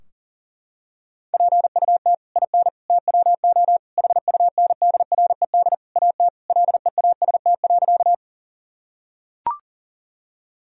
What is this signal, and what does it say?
Morse code, heard as a keyed tone, about 40 wpm, 700 Hz. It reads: PUT IN TWO HUNDRED AT LEAST.
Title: Sentences from Top 1000 Words - Review Part 2 - 40wpm